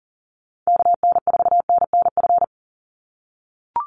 Morse code, heard as a keyed tone, about 40 wpm, 700 Hz. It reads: KN4NNF
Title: USA Call Signs 40wpm